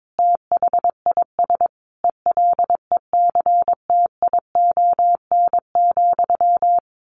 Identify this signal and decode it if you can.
Morse code, heard as a keyed tone, about 22 wpm, 700 Hz. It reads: THIS ELECTION,
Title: Sentences from Top 1500 Words 22wpm